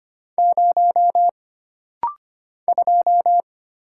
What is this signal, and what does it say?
Morse code, heard as a keyed tone, about 25 wpm, 700 Hz. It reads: 0 2
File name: Single Character (Letter:Number:Punctuation:Prosign) - Rapid-Fire - 25wpm